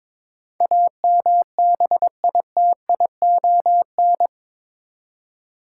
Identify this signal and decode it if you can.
Morse code, heard as a keyed tone, about 22 wpm, 700 Hz. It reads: AMBITION
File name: practice / 22wpm / Eight Letter Words - 22wpm